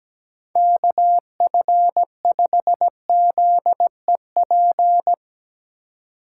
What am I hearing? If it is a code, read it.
Morse code, heard as a keyed tone, about 17 wpm, 700 Hz. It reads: KF5ZEP
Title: USA Call Signs - 6 Characters - 3x Character Spacing - 17wpm